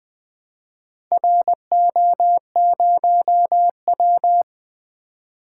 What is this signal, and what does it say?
Morse code, heard as a keyed tone, about 20 wpm, 700 Hz. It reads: RO0W
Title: Four Letter-Number Combinations - 20wpm